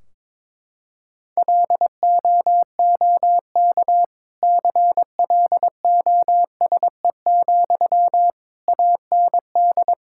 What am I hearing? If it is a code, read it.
Morse code, heard as a keyed tone, about 21 wpm, 700 Hz. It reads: LOOK CLOSE, AND
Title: Sentences from Top 1600 Words 22wpm